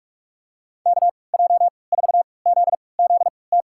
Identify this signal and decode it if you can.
Morse code, heard as a keyed tone, about 47 wpm, spaced as for 26 wpm, 700 Hz. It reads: KJ4ZZT
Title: USA Call Signs - 6 Characters - 3x Character Spacing - 45wpm